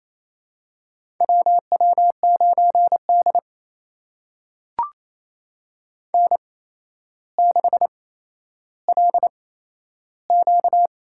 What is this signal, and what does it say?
Morse code, heard as a keyed tone, about 27 wpm, spaced as for 12 wpm, 700 Hz. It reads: WW9D N6LQ
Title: USA Call Signs - 4 Characters - 8x Character Spacing - 28wpm